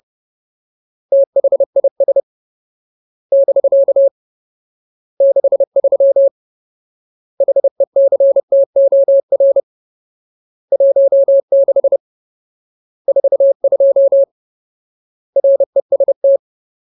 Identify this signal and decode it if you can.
Morse code, heard as a keyed tone, about 30 wpm, 550 Hz. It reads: THIS <BK> 63 HECTOR 16 42 REST
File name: Ultimate Word List / 30WPM Ultimate